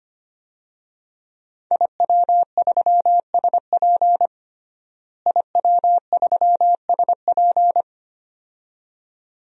Morse code, heard as a keyed tone, about 25 wpm, 700 Hz. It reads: IW3SP IW3SP
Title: DXCC-100 Call Signs - 5 Characters - Repeated 2x - 25wpm